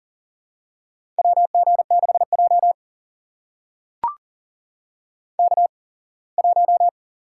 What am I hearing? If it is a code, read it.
Morse code, heard as a keyed tone, about 40 wpm, spaced as for 24 wpm, 700 Hz. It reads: WG6J K1
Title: USA Call Signs - 4 Characters - 8x Character Spacing - 40wpm